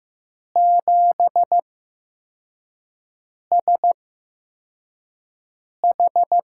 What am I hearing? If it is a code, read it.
Morse code, heard as a keyed tone, about 15 wpm, 700 Hz. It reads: 7 S H
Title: USA Call Signs - 8x Character Spacing - 15wpm